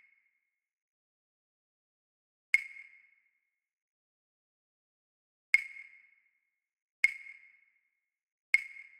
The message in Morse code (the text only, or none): none